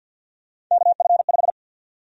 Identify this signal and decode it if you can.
Morse code, heard as a keyed tone, about 47 wpm, 700 Hz. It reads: KF5
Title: USA Call Sign Prefixes - 50wpm